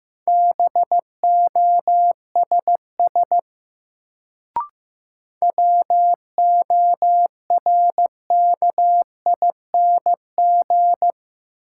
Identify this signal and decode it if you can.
Morse code, heard as a keyed tone, about 15 wpm, 700 Hz. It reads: BOSS WORKING
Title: Sentences from Top 700 Words - New Word Drill - 15wpm